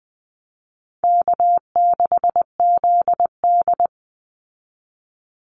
Morse code, heard as a keyed tone, about 20 wpm, 700 Hz. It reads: K6ZD
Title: Call Signs Repeated 2x - 20wpm